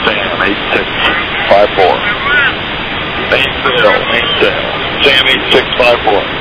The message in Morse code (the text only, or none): none